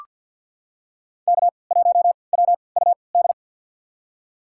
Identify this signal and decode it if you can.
Morse code, heard as a keyed tone, about 47 wpm, spaced as for 28 wpm, 700 Hz. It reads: K1WUD
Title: USA Call Signs - 5 Characters - 3x Character Spacing - 50wpm